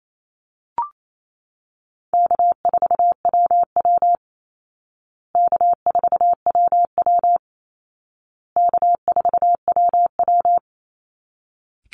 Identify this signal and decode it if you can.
Morse code, heard as a keyed tone, about 28 wpm, 700 Hz. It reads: K4WW K4WW K4WW
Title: Easy Call Signs Repeated 3x - 28wpm